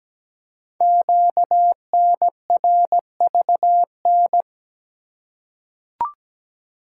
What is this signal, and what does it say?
Morse code, heard as a keyed tone, about 17 wpm, 700 Hz. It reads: QNRVN
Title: Code Groups - 5 Letters - 17wpm